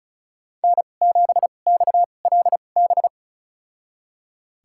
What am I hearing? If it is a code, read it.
Morse code, heard as a keyed tone, about 36 wpm, spaced as for 25 wpm, 700 Hz. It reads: N7XLB
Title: USA Call Signs - 5 Characters - 2x Character Spacing - 35wpm